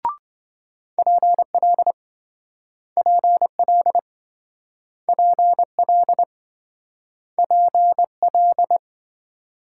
Morse code, heard as a keyed tone, about 26 wpm, 700 Hz. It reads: PL PL PL PL